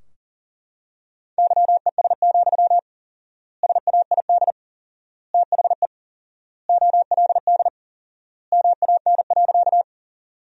Morse code, heard as a keyed tone, about 40 wpm, 700 Hz. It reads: YES, SAID THE OLD MAN.
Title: 4x Word-Spacing - Sentences from Top 500 Words - 40wpm